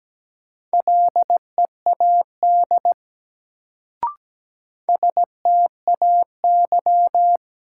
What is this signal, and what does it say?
Morse code, heard as a keyed tone, about 17 wpm, 700 Hz. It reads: LEAD STAY
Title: Sentences from Top 500 Words - New Word Drill - 17wpm